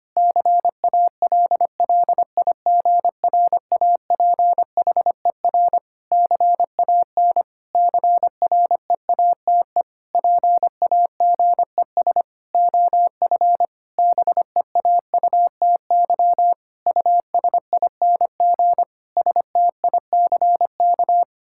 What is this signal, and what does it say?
Morse code, heard as a keyed tone, about 25 wpm, 700 Hz. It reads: CALLIGRAPHER CAN CREATE PAGES OF BEAUTY USING STICK
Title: Pangram Challenge v2 - 25wpm